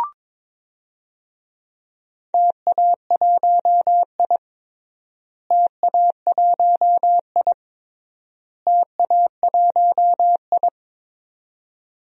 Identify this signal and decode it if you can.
Morse code, heard as a keyed tone, about 22 wpm, 700 Hz. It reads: TA1I TA1I TA1I
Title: DXCC-100 Call Signs - 4 Characters - Repeated 3x - 22wpm